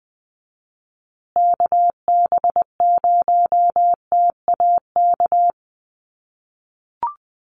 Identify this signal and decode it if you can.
Morse code, heard as a keyed tone, about 20 wpm, 700 Hz. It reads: KB0TAK
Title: USA Call Signs Repeated 3x 20wpm